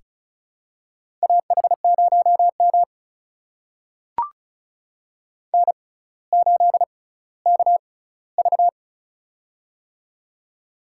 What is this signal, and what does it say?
Morse code, heard as a keyed tone, about 36 wpm, spaced as for 18 wpm, 700 Hz. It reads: AH0M N8KV
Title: USA Call Signs - 4 Characters - 6x Character Spacing - 35wpm